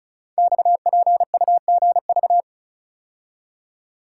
Morse code, heard as a keyed tone, about 36 wpm, 700 Hz. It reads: XPUGV
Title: Code Groups - 5 Letters - 35wpm